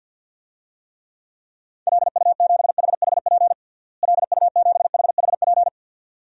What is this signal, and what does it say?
Morse code, heard as a keyed tone, about 53 wpm, 700 Hz. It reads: LU7HHP LU7HHP
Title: DXCC-100 Call Signs - 6 Characters - Repeated 2x - 50wpm